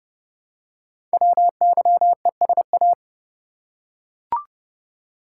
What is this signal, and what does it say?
Morse code, heard as a keyed tone, about 30 wpm, 700 Hz. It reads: WYESA
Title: Code Groups - 5 Letters - 30wpm